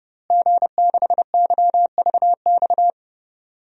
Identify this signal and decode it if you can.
Morse code, heard as a keyed tone, about 32 wpm, 700 Hz. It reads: G6YVX
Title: DXCC-100 Call Signs - 5 Characters - Repeated 3x - 30wpm